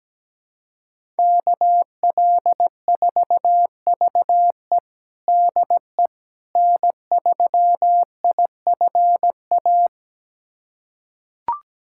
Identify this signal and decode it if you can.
Morse code, heard as a keyed tone, about 17 wpm, 700 Hz. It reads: KL4VE DE N3IFA
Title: USA Callsign DE Callsign - 5 Characters - 17wpm